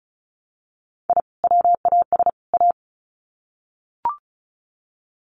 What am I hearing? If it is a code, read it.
Morse code, heard as a keyed tone, about 34 wpm, 700 Hz. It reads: I WAS A
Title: Sets of 3 Words 35wpm